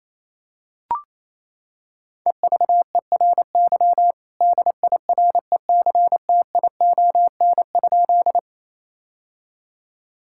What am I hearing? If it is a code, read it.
Morse code, heard as a keyed tone, about 28 wpm, 700 Hz. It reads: EVERY DIRECTION?
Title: Sentences from Top 1000 Words - Review Part 5 - 28wpm